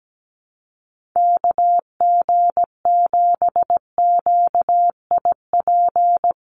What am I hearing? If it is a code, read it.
Morse code, heard as a keyed tone, about 17 wpm, 700 Hz. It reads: KG7QIP
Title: USA Call Signs 17wpm